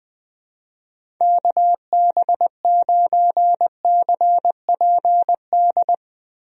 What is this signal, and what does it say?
Morse code, heard as a keyed tone, about 20 wpm, 700 Hz. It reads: KB9CPD
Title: USA Call Signs - 6 Characters - Repeated 3x - 20wpm